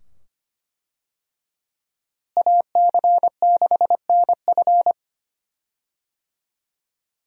Morse code, heard as a keyed tone, about 25 wpm, 700 Hz. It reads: AC6NF
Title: USA Call Signs - 5 Characters - Repeated 2x - 25wpm